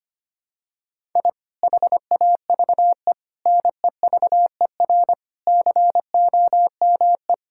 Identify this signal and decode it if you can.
Morse code, heard as a keyed tone, about 25 wpm, 700 Hz. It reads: I HAVE NEVER COME
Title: Sentences from Top 1000 Words - Review Part 5 - 25wpm